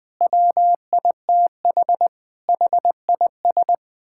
Morse code, heard as a keyed tone, about 20 wpm, 700 Hz. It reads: WITH HIS
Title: Sentences from Top 1000 Words - Review Part 5 - 20wpm